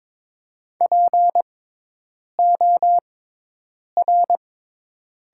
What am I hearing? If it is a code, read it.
Morse code, heard as a keyed tone, about 22 wpm, spaced as for 8 wpm, 700 Hz. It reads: POR
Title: 6x Character Spacing - Top 200 Words - 22wpm